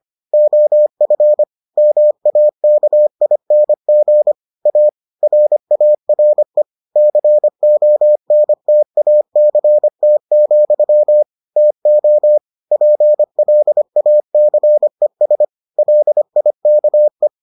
Morse code, heard as a keyed tone, about 25 wpm, 600 Hz. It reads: OF MAKING A RARE CONTACT, TO PLACES LIKE